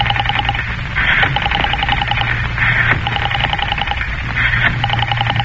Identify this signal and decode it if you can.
Emergency Alert System (SAME) tones, heard as harsh buzzing bursts. It is heard from 0.9 to 4.7 s.